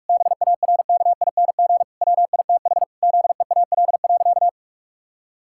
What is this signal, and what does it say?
Morse code, heard as a keyed tone, about 47 wpm, 700 Hz. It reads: BARKING WITH ZEAL.
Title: Pangram Challenge v2 - 45wpm